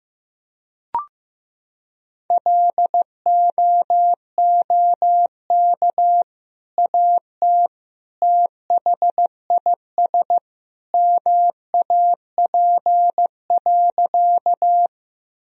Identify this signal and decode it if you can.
Morse code, heard as a keyed tone, about 16 wpm, 700 Hz. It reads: LOOK AT THIS MAP.
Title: Sentences from Top 1000 Words - Review Part 2 - 15wpm